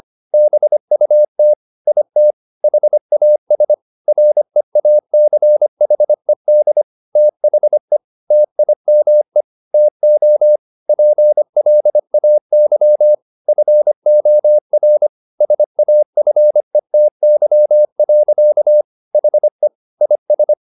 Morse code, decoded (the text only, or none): BUT IT HAS REACHED THE TIME TO PLAY FOR SAFETY. HE IS